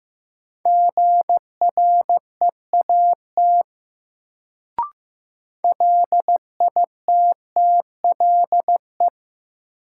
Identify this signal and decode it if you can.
Morse code, heard as a keyed tone, about 15 wpm, 700 Hz. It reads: GREAT LITTLE
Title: Sets of 2 Words - New Word Drill - 15wpm